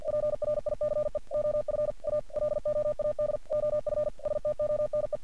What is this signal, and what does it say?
Morse code, heard as a keyed tone, about 48 wpm, 600 Hz. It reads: PRICE OF A LOAD OF STONE
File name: archived / CRSnet